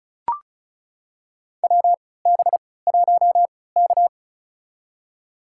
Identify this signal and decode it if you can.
Morse code, heard as a keyed tone, about 36 wpm, spaced as for 20 wpm, 700 Hz. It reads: WB1K
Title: USA Call Signs - 4 Characters - 3x Character Spacing - 35wpm